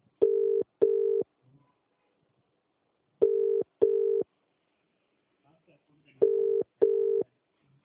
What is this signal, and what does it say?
Ringback tone (UK), 400 + 450 Hz, with speech.